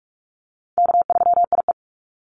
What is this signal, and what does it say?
Morse code, heard as a keyed tone, about 46 wpm, 700 Hz. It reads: K3IE
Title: Call Signs - 45wpm